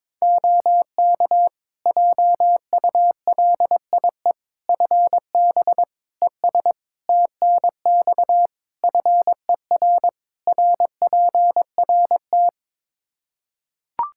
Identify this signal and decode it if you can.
Morse code, heard as a keyed tone, about 22 wpm, 700 Hz. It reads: OK JULIE FB ES TNX FER RPRT
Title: Practice QSO - Exchange 2 - Info - 22wpm